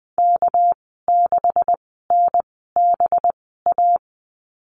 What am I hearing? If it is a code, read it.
Morse code, heard as a keyed tone, about 20 wpm, spaced as for 15 wpm, 700 Hz. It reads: K6NBA
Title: 231 Lesson 37g - Letter x - Call Signs 20wpm